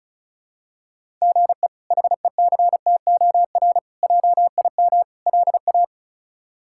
Morse code, heard as a keyed tone, about 34 wpm, 700 Hz. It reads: GE HECTOR JIM LA
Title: K1USN SST QSO Practice - 35wpm